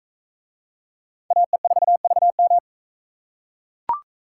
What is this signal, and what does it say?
Morse code, heard as a keyed tone, about 41 wpm, 700 Hz. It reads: AE4VM